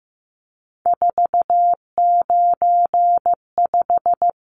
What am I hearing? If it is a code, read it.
Morse code, heard as a keyed tone, about 15 wpm, 700 Hz. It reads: 495
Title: Numbers - 3-Digit 15wpm